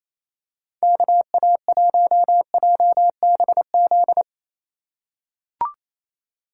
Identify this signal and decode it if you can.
Morse code, heard as a keyed tone, about 28 wpm, 700 Hz. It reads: KA1JBZ
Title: USA Call Signs - 6 Characters - 4x Character Spacing - 28wpm